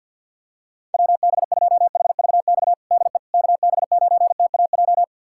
Morse code, heard as a keyed tone, about 48 wpm, 700 Hz. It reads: WB2HVX DE KB9TAJ
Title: USA Callsign DE Callsign - 6 Characters - 50wpm